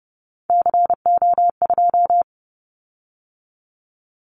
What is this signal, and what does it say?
Morse code, heard as a keyed tone, about 30 wpm, 700 Hz. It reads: CO2
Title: Callsign Prefixes for Top 100 DXCC - 30wpm